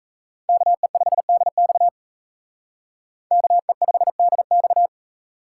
Morse code, heard as a keyed tone, about 35 wpm, 700 Hz. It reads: KE5DX KE5DX
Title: Speed Racing - USA 2x2 Call Signs 28wpm